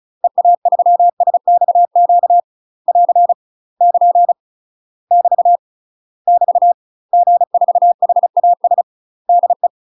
Morse code, heard as a keyed tone, about 36 wpm, 700 Hz. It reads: EA3SXQ <AR> <KN> <BT> <BT> G4HAS DE
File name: UK MASTER QSOs TRAINING / 35WPM / UK MASTER QSOs 0018